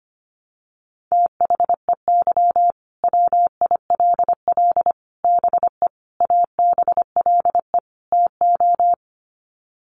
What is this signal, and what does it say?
Morse code, heard as a keyed tone, about 25 wpm, 700 Hz. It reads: THEY WILL BE ABLE TO
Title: Sets of 5 Words Encore 25wpm